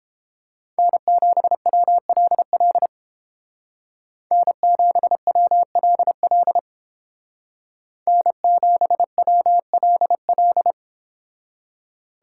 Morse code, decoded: N7WLL N7WLL N7WLL

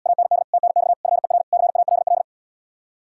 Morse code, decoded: AND TNX FER CALL

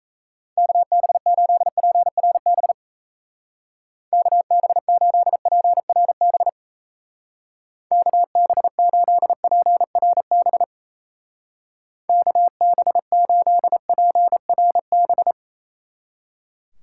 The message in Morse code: KB8PRB KB8PRB KB8PRB KB8PRB

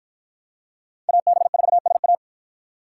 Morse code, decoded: AB4SA

